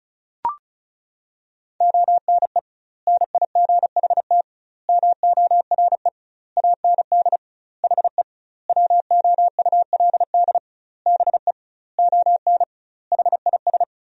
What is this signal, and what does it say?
Morse code, heard as a keyed tone, about 34 wpm, 700 Hz. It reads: ONE NIGHT MORE AND HE WOULD BE ON HIS
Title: 2x Word-Spacing - Sentences from Top 300 Words - 35wpm